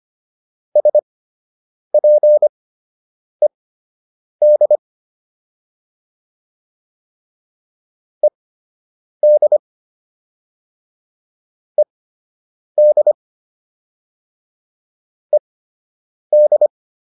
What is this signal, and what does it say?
Morse code, heard as a keyed tone, about 25 wpm, spaced as for 8 wpm, 600 Hz. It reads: SPED ED ED ED